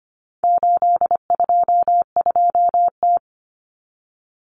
Morse code, heard as a keyed tone, about 25 wpm, 700 Hz. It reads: 822T
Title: Common Cut Numbers - 4-Digit 25wpm